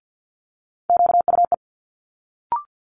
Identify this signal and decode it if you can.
Morse code, heard as a keyed tone, about 50 wpm, 700 Hz. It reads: QUE
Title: Multi-letter Phonemes Quick Fire 50wpm